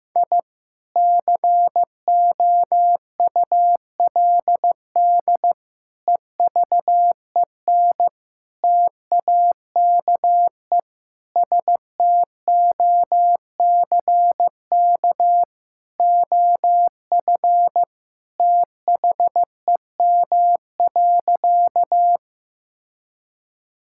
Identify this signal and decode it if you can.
Morse code, heard as a keyed tone, about 15 wpm, 700 Hz. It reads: I COULD EVEN TAKE STOCK OF THEM.
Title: Sentences from Top 1000 Words - Review Part 5 - 15wpm